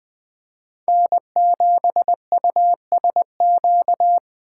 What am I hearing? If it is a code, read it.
Morse code, heard as a keyed tone, about 19 wpm, 700 Hz. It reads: N7USQ